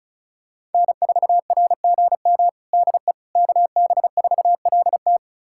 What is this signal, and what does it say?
Morse code, heard as a keyed tone, about 36 wpm, 700 Hz. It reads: N4RGM DE KB4LT
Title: USA Callsign DE Callsign - 5 Characters - 35wpm